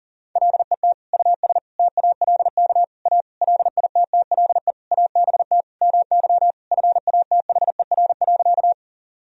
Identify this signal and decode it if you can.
Morse code, heard as a keyed tone, about 40 wpm, 700 Hz. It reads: LET US TALK A LITTLE ABT MY FATHER.
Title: Sentences from Top 400 Words and Abbreviations 40wpm